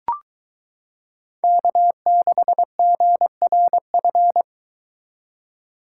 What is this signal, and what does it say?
Morse code, heard as a keyed tone, about 22 wpm, 700 Hz. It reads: K6GRF